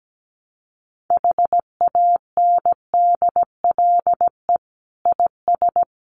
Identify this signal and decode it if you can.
Morse code, heard as a keyed tone, about 17 wpm, 700 Hz. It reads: HANDLE IS
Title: Common QSO Phrases 17wpm